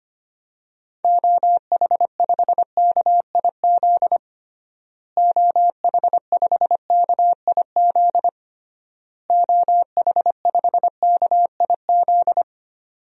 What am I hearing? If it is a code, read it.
Morse code, heard as a keyed tone, about 25 wpm, 700 Hz. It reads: OH5KIZ OH5KIZ OH5KIZ
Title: DXCC-100 Call Signs - 6 Characters - Repeated 3x - 25wpm